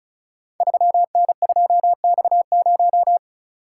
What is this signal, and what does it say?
Morse code, heard as a keyed tone, about 34 wpm, 700 Hz. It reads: 3N2X0
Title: Code Groups - 5 Letter:Numbers - 35wpm